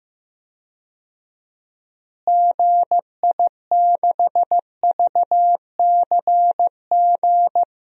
Morse code, heard as a keyed tone, about 15 wpm, 700 Hz. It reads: GI6VCG
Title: DXCC-100 Call Signs - 6 Characters - Repeated 2x - 15wpm